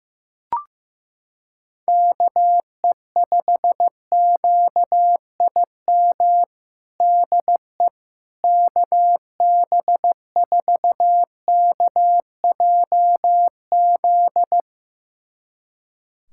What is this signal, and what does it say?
Morse code, heard as a keyed tone, about 15 wpm, 700 Hz. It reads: KE5QIM DE KB4KJZ
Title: USA Callsign DE Callsign - 6 Characters - 15wpm